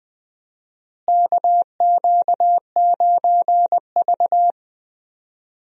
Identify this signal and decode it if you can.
Morse code, heard as a keyed tone, about 21 wpm, 700 Hz. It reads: KQ9V
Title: USA Call Signs - 4 Characters - Repeated 3x - 20wpm